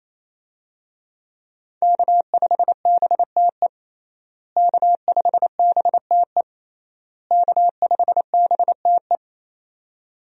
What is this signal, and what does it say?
Morse code, heard as a keyed tone, about 28 wpm, 700 Hz. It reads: K5BTE K5BTE K5BTE
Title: USA Call Signs - 5 Characters - Repeated 3x - 28wpm